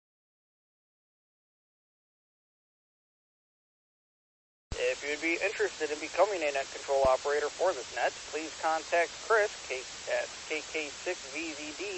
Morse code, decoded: none

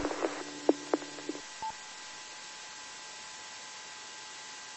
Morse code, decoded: none

